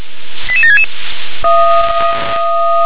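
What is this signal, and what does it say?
Five-tone selective call (ZVEI), heard as quick runs of five tones, tones of about 70 ms. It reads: 88766
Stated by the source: ZVEI five-tone call received on SDR.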